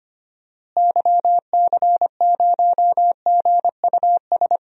Morse code, heard as a keyed tone, about 25 wpm, 700 Hz. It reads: YC0GUS
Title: DXCC-100 Call Signs - 6 Characters - Repeated 3x - 25wpm